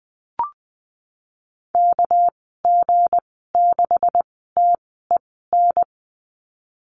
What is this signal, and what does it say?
Morse code, heard as a keyed tone, about 19 wpm, spaced as for 14 wpm, 700 Hz. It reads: KG6TEN
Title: 200 Lesson 32g - Letter k - Call Signs 20wpm